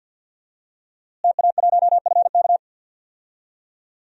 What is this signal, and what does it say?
Morse code, heard as a keyed tone, about 51 wpm, 700 Hz. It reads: TA1FK
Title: DXCC-100 Call Signs - 5 Characters - Repeated 3x - 50wpm